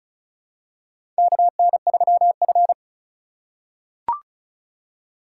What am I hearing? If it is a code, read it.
Morse code, heard as a keyed tone, about 34 wpm, 700 Hz. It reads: KN3F
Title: USA Call Signs - 4 Characters - 4x Character Spacing - 35wpm